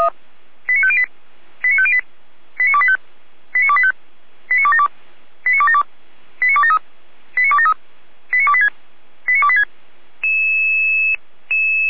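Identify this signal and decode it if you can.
Five-tone selective call (ZVEI), heard as quick runs of five tones, tones of about 70 ms. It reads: 89508 89508 89285 89285 89272 89272 89273 89273 89286 89286
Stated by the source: ZVEI five-tone selective call received on SDR.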